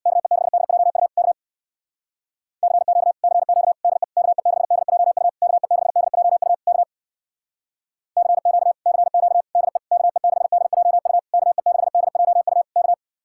Morse code, heard as a keyed tone, about 60 wpm, 700 Hz. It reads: KE6DJU K CQ CQ DE KE6DJU KE6DJU K CQ CQ DE KE6DJU KE6DJU K
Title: Speed-Racing Common QSO Phrases Encore - 50wpm